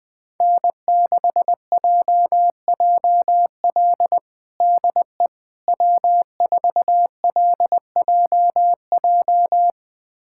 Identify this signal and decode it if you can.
Morse code, heard as a keyed tone, about 20 wpm, 700 Hz. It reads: N6JJL DE W4LJJ
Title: USA Callsign DE Callsign - 5 Characters - 20wpm